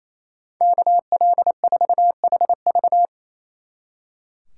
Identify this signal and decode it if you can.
Morse code, heard as a keyed tone, about 33 wpm, 700 Hz. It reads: KL4HV
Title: Speed Racing - USA 2x2 Call Signs 28wpm